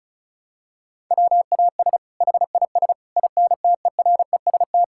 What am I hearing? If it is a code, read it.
Morse code, heard as a keyed tone, about 34 wpm, 700 Hz. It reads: WAS HIS INTEREST
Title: Sentences from Top 1000 Words - Review Part 5 - 35wpm